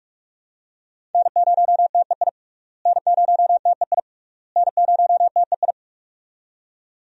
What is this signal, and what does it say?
Morse code, heard as a keyed tone, about 42 wpm, 700 Hz. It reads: N0TEI N0TEI N0TEI
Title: USA Call Signs - 5 Characters - Repeated 3x - 45wpm